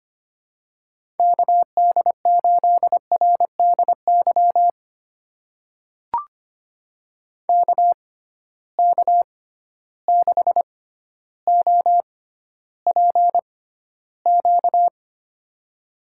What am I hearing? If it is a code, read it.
Morse code, heard as a keyed tone, about 25 wpm, spaced as for 13 wpm, 700 Hz. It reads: KD8RDY KK6OPQ
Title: USA Call Signs - 6 Characters - 6x Character Spacing - 25wpm